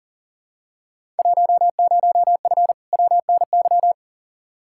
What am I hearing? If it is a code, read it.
Morse code, heard as a keyed tone, about 40 wpm, 700 Hz. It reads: 10F WNY
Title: Field Day Exchanges Encore - 40wpm